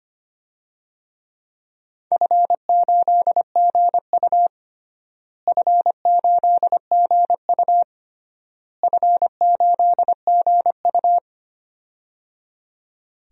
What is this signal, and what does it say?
Morse code, heard as a keyed tone, about 25 wpm, 700 Hz. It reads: F8GU F8GU F8GU
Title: DXCC-100 Call Signs - 4 Characters - Repeated 3x - 25wpm